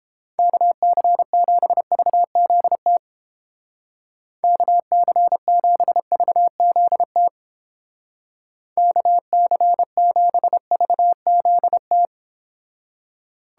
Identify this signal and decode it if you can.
Morse code, heard as a keyed tone, about 29 wpm, 700 Hz. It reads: KC7VZT KC7VZT KC7VZT